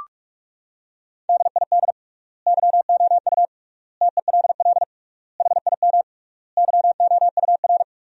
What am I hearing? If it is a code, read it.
Morse code, heard as a keyed tone, about 46 wpm, 700 Hz. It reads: DID YOU TELL HIM YOUR
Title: 3x Word-Spacing - Sentences from Top 400 Words - 45wpm